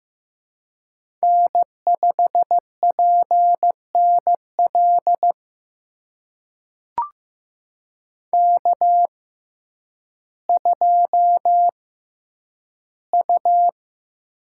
Morse code, heard as a keyed tone, about 15 wpm, spaced as for 9 wpm, 700 Hz. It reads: N5PNL K2U
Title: USA Call Signs - 5 Characters - 6x Character Spacing - 15wpm